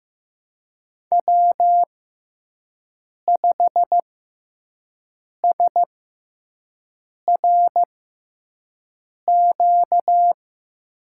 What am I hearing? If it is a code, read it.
Morse code, heard as a keyed tone, about 15 wpm, spaced as for 5 wpm, 700 Hz. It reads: W5SRQ
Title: USA Call Signs - 5 Characters - 6x Character Spacing - 15wpm